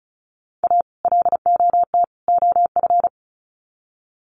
Morse code, heard as a keyed tone, about 34 wpm, 700 Hz. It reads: A LOT OF
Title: Sets of 3 Words 35wpm